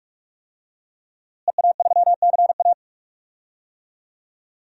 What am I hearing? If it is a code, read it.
Morse code, heard as a keyed tone, about 46 wpm, 700 Hz. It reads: EA3CA
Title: DXCC-100 Call Signs - 5 Characters - Repeated 3x - 45wpm